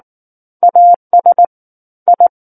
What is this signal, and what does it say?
Morse code, heard as a keyed tone, about 19 wpm, 700 Hz. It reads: AS I
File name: Book One / 10-Chapter 10 - 19 WPM